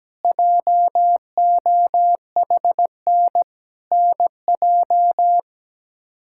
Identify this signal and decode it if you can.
Morse code, heard as a keyed tone, about 17 wpm, 700 Hz. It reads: JOHN NJ
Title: K1USN SST QSO Practice - 17wpm